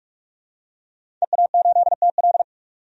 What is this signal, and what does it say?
Morse code, heard as a keyed tone, about 44 wpm, 690 Hz. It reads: EA8TL
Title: DXCC-100 Call Signs - 5 Characters - 45wpm